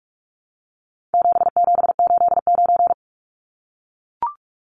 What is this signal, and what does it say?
Morse code, heard as a keyed tone, about 44 wpm, 700 Hz. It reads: 7789
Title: Numbers - 4-Digit 45wpm